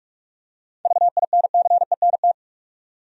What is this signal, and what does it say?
Morse code, heard as a keyed tone, about 46 wpm, 700 Hz. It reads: VINCENT
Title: USA Common Names - 45wpm